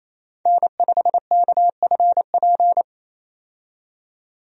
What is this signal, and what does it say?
Morse code, heard as a keyed tone, about 28 wpm, 700 Hz. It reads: N5KFP
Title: USA Call Signs - 5 Characters - 28wpm